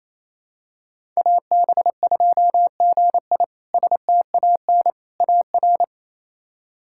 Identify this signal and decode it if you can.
Morse code, heard as a keyed tone, about 28 wpm, 700 Hz. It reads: AB2GI STAN AR
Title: K1USN SST QSO Practice - 28wpm